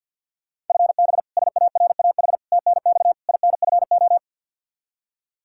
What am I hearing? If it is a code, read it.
Morse code, heard as a keyed tone, about 52 wpm, 700 Hz. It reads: FB SARAH TNX INFO
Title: K1USN SST QSO Practice 50wpm